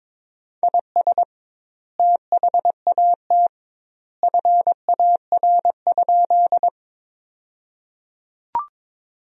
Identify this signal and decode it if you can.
Morse code, heard as a keyed tone, about 22 wpm, 700 Hz. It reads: IS THAT FAR?
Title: 2x Word-Spacing - Sentences from Top 300 Words - 22wpm